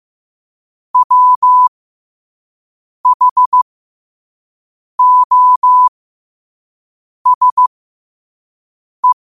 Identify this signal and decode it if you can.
Morse code, heard as a keyed tone, about 15 wpm, spaced as for 5 wpm, 1000 Hz. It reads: WHOSE
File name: qotdfiles / 5wpm